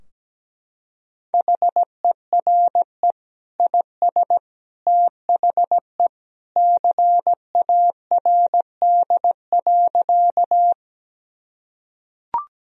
Morse code, heard as a keyed tone, about 17 wpm, 700 Hz. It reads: HERE IS THE CARD.
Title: Sentences from Top 1000 Words - Review Part 4 - 17wpm